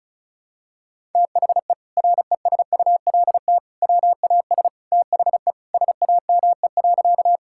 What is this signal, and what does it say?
Morse code, heard as a keyed tone, about 34 wpm, 700 Hz. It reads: THE RESULT WAS THE SAME.
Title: Sentences from Top 1000 Words - Review Part 5 - 35wpm